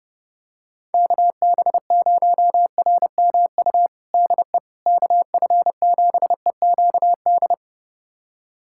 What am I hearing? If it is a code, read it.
Morse code, heard as a keyed tone, about 32 wpm, 700 Hz. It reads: KB0RMU DE KF7EQD
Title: USA Callsign DE Callsign - 6 Characters - 30wpm